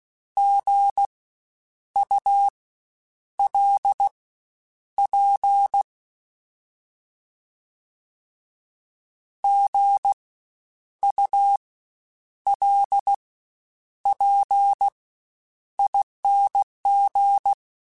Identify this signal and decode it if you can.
Morse code, heard as a keyed tone, about 16 wpm, spaced as for 8 wpm, 800 Hz. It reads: GULP GULPING
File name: Spaced Alphas / Spaced Alphas mp3s / Gs sp @15